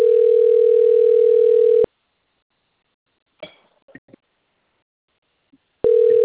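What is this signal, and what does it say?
Ringback tone (North America), 440 + 480 Hz.
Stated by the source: Telephone call progress tone recorded on the Cornelio Procopio system.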